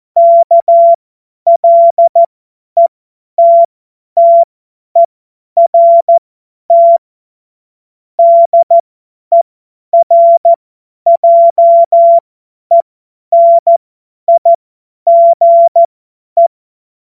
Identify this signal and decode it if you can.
Morse code, heard as a keyed tone, about 14 wpm, spaced as for 10 wpm, 680 Hz. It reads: KLETTERT DERJENIGE
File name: Morsekurs / Witze / Witz 28. Dezember 2021, 50 F70